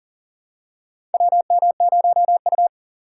Morse code, heard as a keyed tone, about 40 wpm, 700 Hz. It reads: WM0U
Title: USA Call Signs - 2x Character Spacing - 40wpm